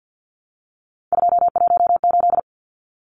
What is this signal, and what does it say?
Morse code, heard as a keyed tone, about 50 wpm, 700 Hz. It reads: 218
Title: Numbers - 3-Digit 50wpm